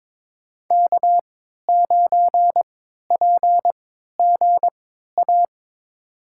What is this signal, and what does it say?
Morse code, heard as a keyed tone, about 22 wpm, spaced as for 12 wpm, 700 Hz. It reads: K9PGA
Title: USA Call Signs - 3x Character Spacing - 22wpm